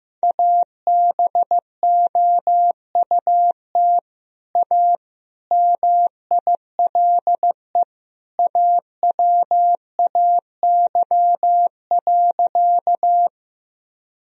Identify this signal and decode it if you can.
Morse code, heard as a keyed tone, about 15 wpm, 700 Hz. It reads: ABOUT A MILE AWAY.
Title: Sentences from Top 1000 Words - Review Part 1 - 15wpm